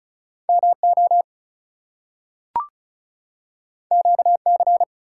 Morse code, heard as a keyed tone, about 36 wpm, 700 Hz. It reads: MO QC
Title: US States and Canadian Provinces 35wpm